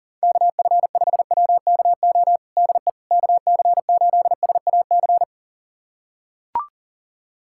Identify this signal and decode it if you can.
Morse code, heard as a keyed tone, about 37 wpm, 700 Hz. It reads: KF5WKO DE KC8SAC
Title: USA Callsign DE Callsign 40wpm